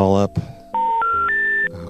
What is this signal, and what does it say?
Special information tone, with speech.